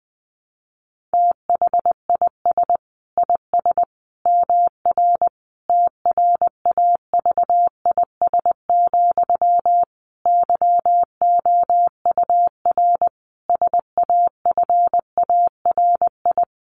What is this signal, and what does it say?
Morse code, heard as a keyed tone, about 20 wpm, 700 Hz. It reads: THIS IS MR TRAVIS, YOUR SAFARI
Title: A Sound Of Thunder 20wpm